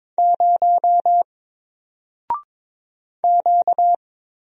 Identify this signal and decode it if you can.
Morse code, heard as a keyed tone, about 22 wpm, 700 Hz. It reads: 0 Q
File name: Single-Letter-Number-22wpm